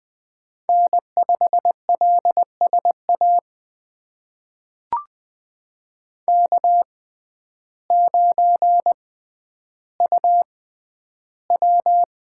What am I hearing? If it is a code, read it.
Morse code, heard as a keyed tone, about 21 wpm, spaced as for 11 wpm, 700 Hz. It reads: N5LSA K9UW
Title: USA Call Signs - 5 Characters - 6x Character Spacing - 20wpm